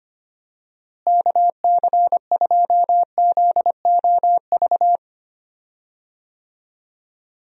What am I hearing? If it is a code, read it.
Morse code, heard as a keyed tone, about 25 wpm, 700 Hz. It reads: KC2ZOV
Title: USA Call Signs - 6 Characters - Repeated 2x - 25wpm